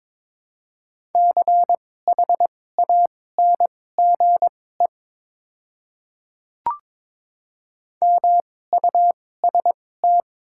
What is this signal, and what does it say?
Morse code, heard as a keyed tone, about 22 wpm, spaced as for 16 wpm, 700 Hz. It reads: CHANGE MUST
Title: 2x Character Spacing - Top 200 Words - 22wpm